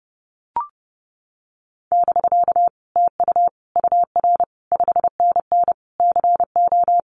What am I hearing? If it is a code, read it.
Morse code, heard as a keyed tone, about 30 wpm, 700 Hz. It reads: <BK> TU UR 5NN CO
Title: Common QSO Phrases Encore 30wpm